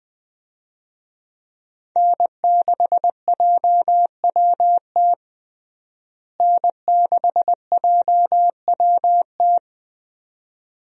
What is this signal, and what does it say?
Morse code, heard as a keyed tone, about 20 wpm, 700 Hz. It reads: N6JWT N6JWT
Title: USA Call Signs - 5 Characters - Repeated 2x - 20wpm